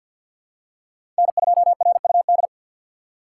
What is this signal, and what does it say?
Morse code, heard as a keyed tone, about 52 wpm, 700 Hz. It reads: N2RUD
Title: USA Call Signs - 4x Character Spacing - 50wpm